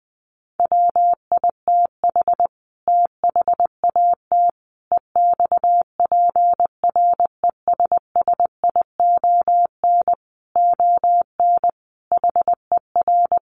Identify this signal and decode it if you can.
Morse code, heard as a keyed tone, about 20 wpm, 700 Hz. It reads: WITH THAT EXPRESSION ON HER
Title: Sentences from Top 1200 Words 20wpm